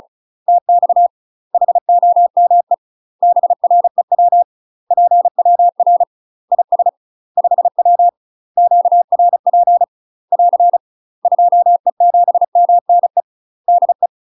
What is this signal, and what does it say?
Morse code, heard as a keyed tone, about 36 wpm, 700 Hz. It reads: TX HOME BREW PWR IS 5W QRP <AR> 2E7MNE DE
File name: UK MASTER QSOs TRAINING / 35WPM / UK MASTER QSOs 0031